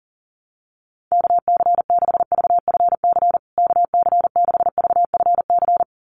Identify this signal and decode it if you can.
Morse code, heard as a keyed tone, about 40 wpm, 700 Hz. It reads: KC6VFC KC6VFC
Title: USA Call Signs Repeated 2x 40wpm